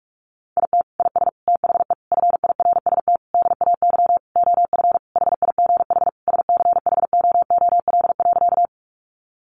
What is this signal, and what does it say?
Morse code, heard as a keyed tone, about 44 wpm, 700 Hz. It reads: IT IS THE FIRST DAY OF HIGH SCHOOL.
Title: Additional Easy Phrases 45wpm